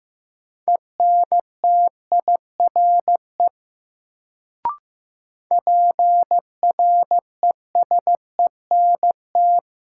Morse code, heard as a keyed tone, about 15 wpm, 700 Hz. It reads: ENTIRE PRESENT
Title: Sentences from Top 800 Words - New Word Drill - 15wpm